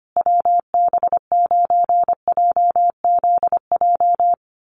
Morse code, heard as a keyed tone, about 25 wpm, 700 Hz. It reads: WB9JZJ
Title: USA Call Signs Repeated 3x 25wpm